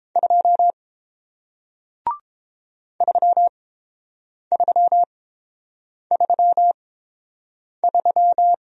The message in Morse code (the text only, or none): 2 3333